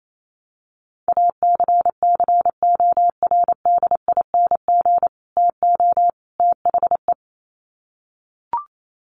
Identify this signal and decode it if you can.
Morse code, heard as a keyed tone, about 28 wpm, 700 Hz. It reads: ACCORDING TO THE
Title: Sets of 3 Words 28wpm